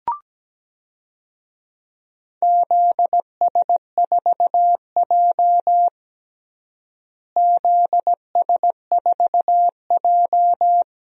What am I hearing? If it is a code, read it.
Morse code, heard as a keyed tone, about 17 wpm, 700 Hz. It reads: ZS4J ZS4J
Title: DXCC-100 Call Signs - 4 Characters - Repeated 3x - 17wpm